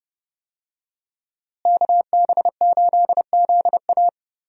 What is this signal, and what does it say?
Morse code, heard as a keyed tone, about 32 wpm, 700 Hz. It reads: KB8ZA
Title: USA Call Signs - 5 Characters - Repeated 3x - 30wpm